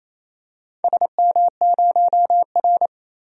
Morse code, heard as a keyed tone, about 29 wpm, 700 Hz. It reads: SM0R
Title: DXCC-100 Call Signs - 4 Characters - 28wpm